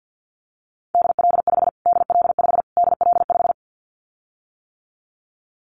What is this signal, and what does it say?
Morse code, heard as a keyed tone, about 50 wpm, 700 Hz. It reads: DL5 DL5 DL5
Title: Callsign Prefixes for Top 25 DXCC - Repeated 3x - 50wpm